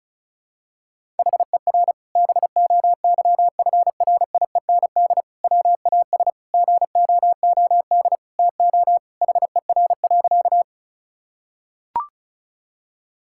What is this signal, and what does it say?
Morse code, heard as a keyed tone, about 36 wpm, 700 Hz. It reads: HER BOYFRIEND WAS GOOD TO HER.
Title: Sentences from Top 1000 Words - Review Part 5 - 35wpm